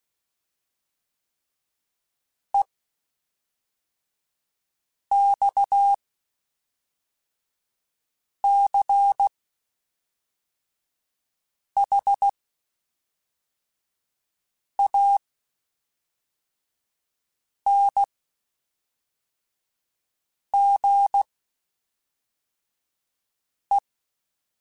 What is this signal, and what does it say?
Morse code, heard as a keyed tone, about 16 wpm, spaced as for 3 wpm, 800 Hz. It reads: EXCHANGE
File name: (spaced) / QSOs Spaced-mp3s / WA7QOX 1 @5 sp